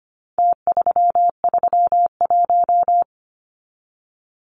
Morse code, heard as a keyed tone, about 25 wpm, 700 Hz. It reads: T331
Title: Numbers - Contest Style - 4-Digit 25wpm